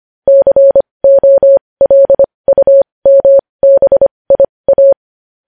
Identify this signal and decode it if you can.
Morse code, heard as a keyed tone, about 25 wpm, spaced as for 20 wpm, 550 Hz. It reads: COLUMBIA
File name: podcast / News headlines-1x-25x20